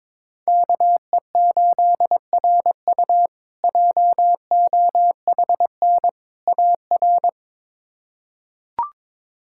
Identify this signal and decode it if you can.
Morse code, heard as a keyed tone, about 22 wpm, 700 Hz. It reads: KE8RU JOHN AR
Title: K1USN SST QSO Practice - 22wpm